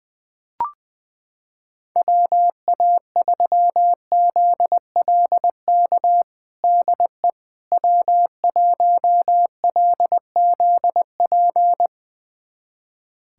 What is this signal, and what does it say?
Morse code, heard as a keyed tone, about 20 wpm, 700 Hz. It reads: WA3ZLK DE W1LZP
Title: USA Callsign DE Callsign 20wpm